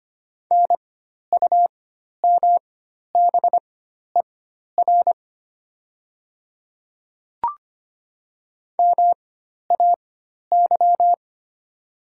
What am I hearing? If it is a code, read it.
Morse code, heard as a keyed tone, about 25 wpm, spaced as for 12 wpm, 700 Hz. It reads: NUMBER MAY
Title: 4x Character Spacing - Top 100 Words - 25wpm